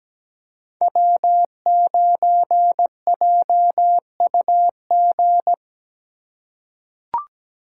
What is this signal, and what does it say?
Morse code, heard as a keyed tone, about 17 wpm, 700 Hz. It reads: W9JUG
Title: K1USN SST QSO Practice - 17wpm